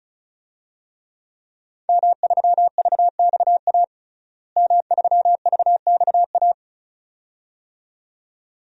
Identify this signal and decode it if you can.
Morse code, heard as a keyed tone, about 34 wpm, 700 Hz. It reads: M3VXA M3VXA
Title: DXCC-100 Call Signs - 5 Characters - Repeated 2x - 35wpm